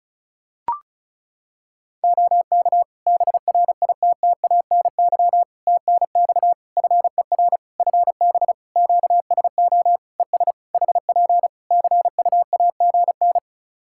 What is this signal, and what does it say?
Morse code, heard as a keyed tone, about 34 wpm, 700 Hz. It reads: OK BRITTANY TNX FER FB QSO ES HP CUAGN
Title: Practice QSO - Exchange 4 - Ending - 35wpm